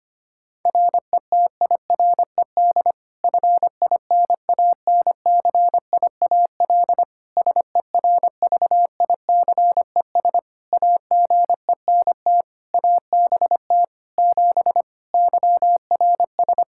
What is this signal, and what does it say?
Morse code, heard as a keyed tone, about 25 wpm, 700 Hz. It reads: RETIRED FINANCIAL SERVICES AGENT ABT 7 YRS
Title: K1USN SST QSO Practice 25wpm